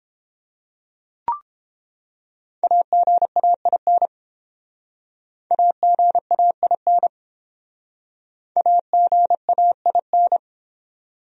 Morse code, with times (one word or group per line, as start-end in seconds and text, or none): 2.6-4.1 AGAIN
5.5-7.1 AGAIN
8.6-10.4 AGAIN